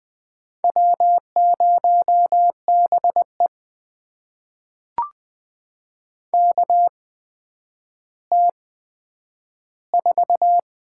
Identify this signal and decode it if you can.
Morse code, heard as a keyed tone, about 20 wpm, spaced as for 10 wpm, 700 Hz. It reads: W0BE KT4
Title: USA Call Signs - 8x Character Spacing - 20wpm